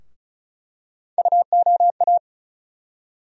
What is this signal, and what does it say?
Morse code, heard as a keyed tone, about 36 wpm, 700 Hz. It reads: UOA
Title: USA Call Sign Postfixes - 3 Letters 35wpm